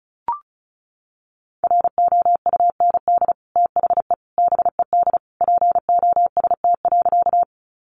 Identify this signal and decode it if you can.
Morse code, heard as a keyed tone, about 34 wpm, 700 Hz. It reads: ROUND THE BED POST.
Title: Sentences from Top 1000 Words 35wpm